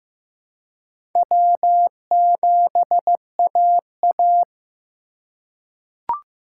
Morse code, heard as a keyed tone, about 15 wpm, 700 Hz. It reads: W7AA
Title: USA Call Signs - 3x Character Spacing - 15wpm